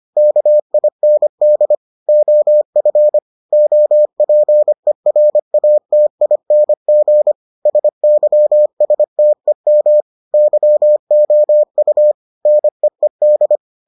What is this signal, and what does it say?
Morse code, heard as a keyed tone, about 25 wpm, 600 Hz. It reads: KIND OF OPERATING SYSTEM YOU NEED